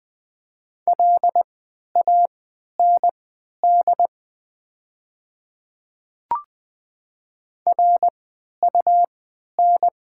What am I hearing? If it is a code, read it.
Morse code, heard as a keyed tone, about 20 wpm, spaced as for 11 wpm, 700 Hz. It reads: LAND RUN
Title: 3x Character Spacing - Top 500 Words - 20wpm